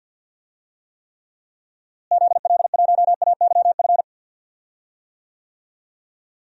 Morse code, heard as a keyed tone, about 52 wpm, 700 Hz. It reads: ZL1AYF
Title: DXCC-100 Call Signs - 6 Characters - Repeated 3x - 50wpm